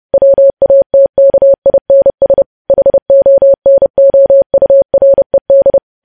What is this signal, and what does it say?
Morse code, heard as a keyed tone, about 30 wpm, 550 Hz. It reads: WATKINS HONOURED
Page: news headlines in morse code at 30 wpm